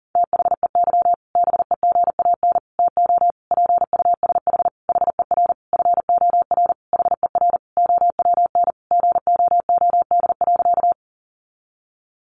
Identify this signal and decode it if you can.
Morse code, heard as a keyed tone, about 40 wpm, 700 Hz. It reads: THEY BEGAN TO PUSH HER FOR HER OWN GOOD.
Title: Sentences from Top 1300 Words 40wpm